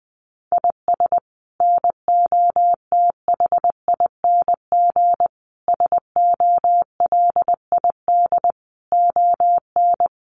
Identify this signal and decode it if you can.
Morse code, heard as a keyed tone, about 20 wpm, 700 Hz. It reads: IS NOTHING SOLID ON
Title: Sentences from Top 1500 Words 20wpm